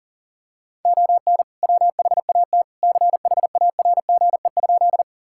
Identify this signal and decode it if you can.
Morse code, heard as a keyed tone, about 37 wpm, 700 Hz. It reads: ON WHAT CHARGE?
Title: Sentences from Top 1000 Words - Review Part 6 - 40wpm